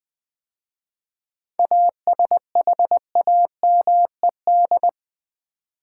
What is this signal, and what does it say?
Morse code, heard as a keyed tone, about 20 wpm, 700 Hz. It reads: ASHAMED
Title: 7 Letter Words - First 3 Letters Given - 20wpm